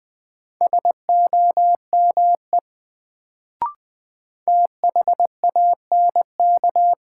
Morse code, heard as a keyed tone, about 20 wpm, 700 Hz. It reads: SOME THANK
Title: Sets of 2 Words - New Word Drill - 20wpm